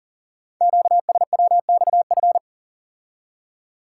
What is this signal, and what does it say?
Morse code, heard as a keyed tone, about 42 wpm, 700 Hz. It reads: QSWXF
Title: Code Groups - 5 Letters - 40wpm